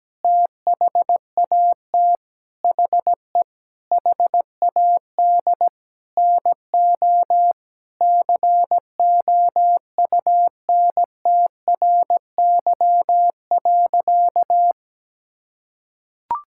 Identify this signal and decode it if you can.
Morse code, heard as a keyed tone, about 17 wpm, 700 Hz. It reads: THAT HE HAD NO COUNTRY.
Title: Sentences from Top 1000 Words - Review Part 5 - 17wpm